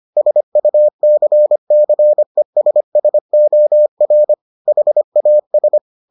Morse code, heard as a keyed tone, about 25 wpm, 600 Hz. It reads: SUCCESSOR HAS